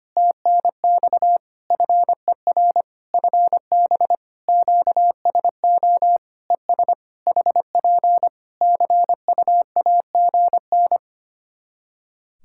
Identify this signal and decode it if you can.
Morse code, heard as a keyed tone, about 25 wpm, 700 Hz. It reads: TNX FER FB QSO ES HP CUAGN
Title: Practice QSO - Exchange 4 - Ending - 25wpm